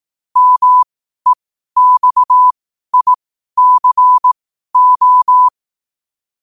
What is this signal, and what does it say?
Morse code, heard as a keyed tone, about 18 wpm, spaced as for 12 wpm, 1000 Hz. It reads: MEXICO
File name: qotdfiles / 12wpm